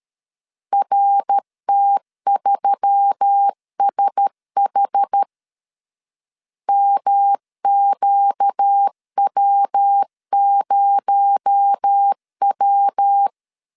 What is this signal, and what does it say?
Morse code, heard as a keyed tone, about 13 wpm, 800 Hz. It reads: RT3SH MQW0W
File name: CW / DET / 1 6 1500C S A 12wpm 3440